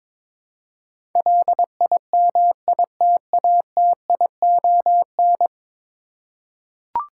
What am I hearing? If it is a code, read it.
Morse code, heard as a keyed tone, about 23 wpm, 700 Hz. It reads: LIMITATION
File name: practice / 22wpm / 10 Letter Words - First 3 Letters Given - 22wpm